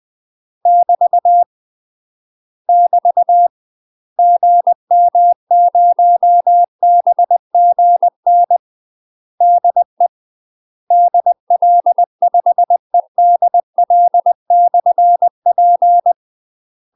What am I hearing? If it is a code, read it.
Morse code, heard as a keyed tone, about 20 wpm, 700 Hz. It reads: <BT> <BT> GM0BGN DE DL5EDL/P
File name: UK MASTER QSOs TRAINING / 20WPM / UK MASTER QSOs 0020